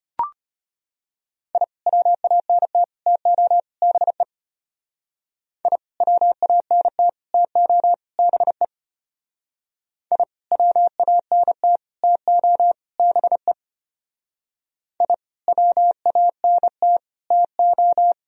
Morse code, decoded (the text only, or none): I WANT TO BE I WANT TO BE I WANT TO BE I WANT TO